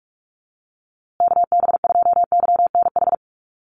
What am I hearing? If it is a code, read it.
Morse code, heard as a keyed tone, about 44 wpm, 700 Hz. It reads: KB2YNH
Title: USA Call Signs Repeated 3x 45wpm